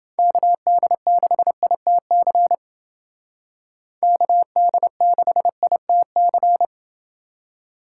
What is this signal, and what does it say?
Morse code, heard as a keyed tone, about 28 wpm, 700 Hz. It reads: KD6ITC KD6ITC